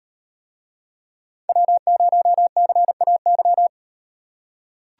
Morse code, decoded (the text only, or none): W0CAY